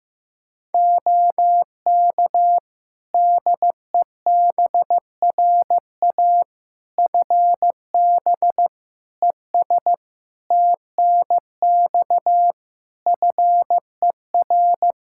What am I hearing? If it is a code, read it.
Morse code, heard as a keyed tone, about 15 wpm, 700 Hz. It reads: OK DEBRA FB ES TNX FER
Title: Practice QSO - Exchange 2 - Info - 15wpm